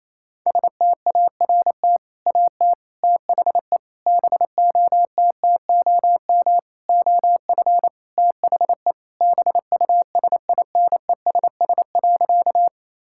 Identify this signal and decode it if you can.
Morse code, heard as a keyed tone, about 28 wpm, 700 Hz. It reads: START AT THE BOTTOM OF THE BUSINESS.
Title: Sentences from Top 1000 Words - Review Part 6 - 28wpm